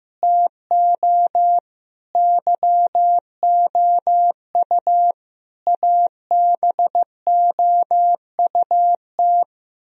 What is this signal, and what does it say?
Morse code, heard as a keyed tone, about 15 wpm, 700 Hz. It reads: TO YOU ABOUT
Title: Sentences from Top 1000 Words - Review Part 1 - 15wpm